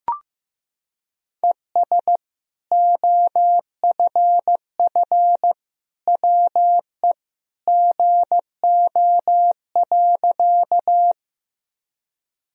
Morse code, heard as a keyed tone, about 16 wpm, 700 Hz. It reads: ES OFF WE GO.
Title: Sentences from Top 200 Words and Abbreviations 15wpm